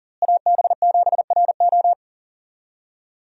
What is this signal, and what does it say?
Morse code, heard as a keyed tone, about 39 wpm, 700 Hz. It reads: AB7RO
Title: USA Call Signs - 5 Characters - Repeated 3x - 40wpm